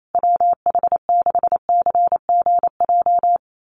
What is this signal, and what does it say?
Morse code, heard as a keyed tone, about 28 wpm, 700 Hz. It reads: WH6CGJ
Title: USA Call Signs Repeated 3x 28wpm